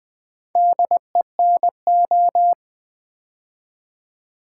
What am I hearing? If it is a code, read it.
Morse code, heard as a keyed tone, about 21 wpm, 700 Hz. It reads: DENO
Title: Build A Word - 4 Letters At A Time - 8-Letter Words - 20wpm